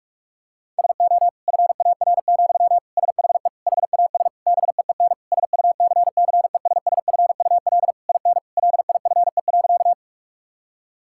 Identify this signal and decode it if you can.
Morse code, heard as a keyed tone, about 46 wpm, 700 Hz. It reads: SO FAR, SHE HAS BEEN SUCCESSFUL IN LIFE.
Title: Sentences from Top 1000 Words - Review Part 6 - 45wpm